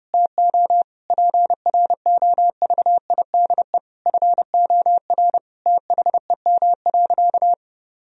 Morse code, heard as a keyed tone, about 32 wpm, 700 Hz. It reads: TO PROVIDE FOR THEM.
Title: Sentences from Top 1000 Words - Review Part 3 - 30wpm